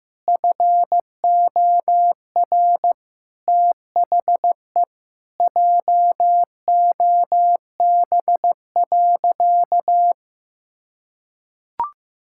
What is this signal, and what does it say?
Morse code, heard as a keyed tone, about 16 wpm, 700 Hz. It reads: FOR THE JOB.
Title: Sentences from Top 1600 Words 15wpm